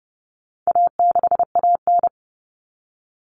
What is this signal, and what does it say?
Morse code, heard as a keyed tone, about 30 wpm, 700 Hz. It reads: A6AN
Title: Common Cut Numbers - 4-Digit 30wpm